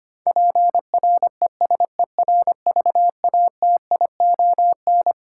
Morse code, heard as a keyed tone, about 25 wpm, 700 Hz. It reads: PRESERVATION
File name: Twelve Letter Words - 25wpm